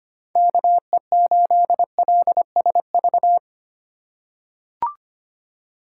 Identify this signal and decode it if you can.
Morse code, heard as a keyed tone, about 25 wpm, 700 Hz. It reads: KE8LSV
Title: USA Call Signs - 3x Character Spacing - 25wpm